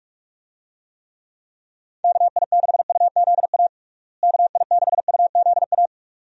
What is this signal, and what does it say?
Morse code, heard as a keyed tone, about 46 wpm, 700 Hz. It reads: KI6UZA KI6UZA
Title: USA Call Signs - 6 Characters - Repeated 2x - 45wpm